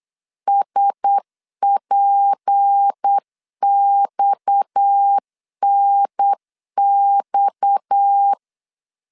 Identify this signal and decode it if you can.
Morse code, heard as a keyed tone, about 8 wpm, 800 Hz. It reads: SPXNX
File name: LEZIONI / 6 PKX0 / 6 PKX 8wpm 3041